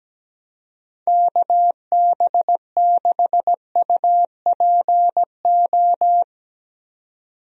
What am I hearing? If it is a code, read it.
Morse code, heard as a keyed tone, about 17 wpm, 700 Hz. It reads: KB6UPO
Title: USA Call Signs - 6 Characters - 17wpm